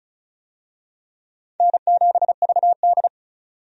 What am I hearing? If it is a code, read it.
Morse code, heard as a keyed tone, about 36 wpm, 700 Hz. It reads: N7VD
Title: USA Call Signs - 4 Characters - 35wpm